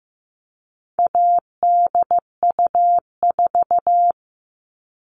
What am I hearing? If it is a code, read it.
Morse code, heard as a keyed tone, about 15 wpm, 700 Hz. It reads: ADU4
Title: All Cut Numbers - 4-Digit 15wpm